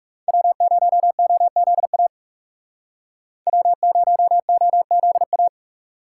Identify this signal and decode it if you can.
Morse code, heard as a keyed tone, about 37 wpm, 700 Hz. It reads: W0OZA W0OZA
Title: Speed Racing - USA 1x3 Call Signs 30wpm